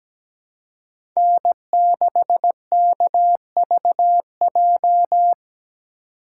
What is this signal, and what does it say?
Morse code, heard as a keyed tone, about 17 wpm, 700 Hz. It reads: N6KVJ
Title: USA Call Signs - 3x Character Spacing - 17wpm